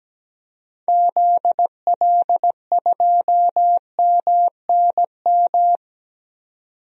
Morse code, heard as a keyed tone, about 17 wpm, 700 Hz. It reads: ZL2MNM
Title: DXCC-100 Call Signs - 6 Characters - 17wpm